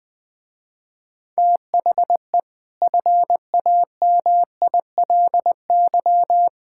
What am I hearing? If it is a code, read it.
Morse code, heard as a keyed tone, about 20 wpm, 700 Hz. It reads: THE FAMILY
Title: Easy Sentences with Hints - 20wpm